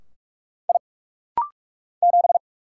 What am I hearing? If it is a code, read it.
Morse code, heard as a keyed tone, about 46 wpm, 700 Hz. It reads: I 7
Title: Single Letter-Number - Rapid-Fire 45wpm